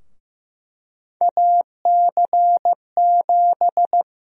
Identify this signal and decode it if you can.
Morse code, heard as a keyed tone, about 15 wpm, 700 Hz. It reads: AC7
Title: USA Call Sign Prefixes - 15wpm